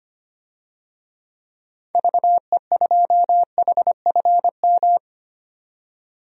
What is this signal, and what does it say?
Morse code, heard as a keyed tone, about 25 wpm, 700 Hz. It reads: VE2HFM
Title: DXCC-100 Call Signs - 6 Characters - 25wpm